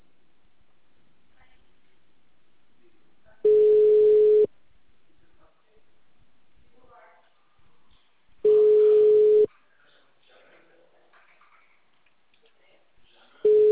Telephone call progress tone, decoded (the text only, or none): ringback tone